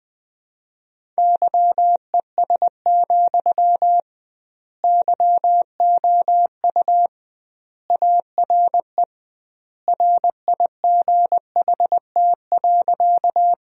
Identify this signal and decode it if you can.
Morse code, heard as a keyed tone, about 20 wpm, 700 Hz. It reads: YES, YOU ARE RIGHT.
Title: 2x Word-Spacing - Sentences from Top 500 Words - 20wpm